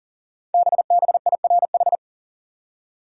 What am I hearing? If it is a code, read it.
Morse code, heard as a keyed tone, about 39 wpm, 700 Hz. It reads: BBIRH
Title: Code Groups - 5 Letters - 40wpm